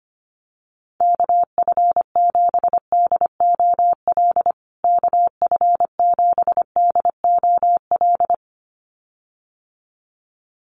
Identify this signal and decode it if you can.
Morse code, heard as a keyed tone, about 25 wpm, 700 Hz. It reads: KF7DOL KF7DOL
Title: USA Call Signs Repeated 2x 25wpm